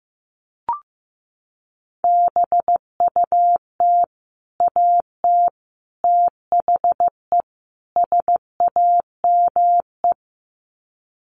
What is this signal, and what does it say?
Morse code, heard as a keyed tone, about 15 wpm, 700 Hz. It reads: BUT AT THE SAME
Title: Sets of 4 Words 15wpm